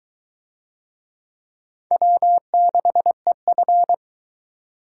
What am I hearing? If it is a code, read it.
Morse code, heard as a keyed tone, about 20 wpm, 700 Hz. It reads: W6EF